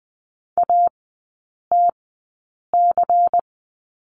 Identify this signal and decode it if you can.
Morse code, heard as a keyed tone, about 20 wpm, spaced as for 8 wpm, 690 Hz. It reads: ATC
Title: 159 Lesson 26e - Letter g - Three Characters 20wpm